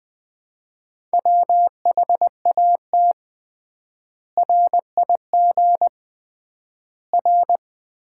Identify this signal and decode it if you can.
Morse code, heard as a keyed tone, about 20 wpm, 700 Hz. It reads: WHAT RIG R
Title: Common QSO Phrases - 3x Word-Spacing - 20wpm